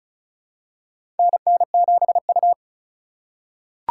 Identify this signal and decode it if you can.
Morse code, heard as a keyed tone, about 36 wpm, 700 Hz. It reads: NN7U
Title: USA Call Signs - 4 Characters - 2x Character Spacing - 35wpm